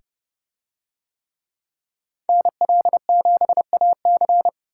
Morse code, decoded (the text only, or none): NL7AC